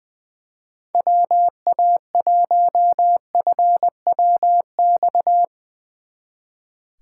Morse code, heard as a keyed tone, about 22 wpm, 700 Hz. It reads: WA1FWX
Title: Speed Racing - USA 2x3 Call Signs 20wpm